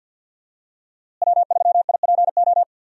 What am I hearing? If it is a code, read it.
Morse code, heard as a keyed tone, about 52 wpm, 700 Hz. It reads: W3IPO
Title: USA Call Signs - 5 Characters - Repeated 3x - 50wpm